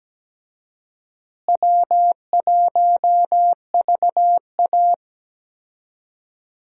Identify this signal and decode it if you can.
Morse code, heard as a keyed tone, about 17 wpm, 700 Hz. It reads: W1VA
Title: USA Call Signs - 4 Characters - Repeated 3x - 17wpm